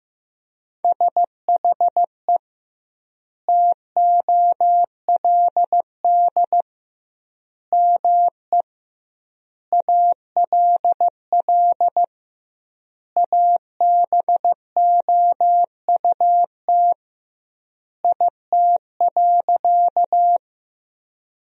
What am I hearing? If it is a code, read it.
Morse code, heard as a keyed tone, about 15 wpm, 700 Hz. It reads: SHE TOLD ME ALL ABOUT IT.
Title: 2x Word-Spacing - Sentences from Top 400 Words - 15wpm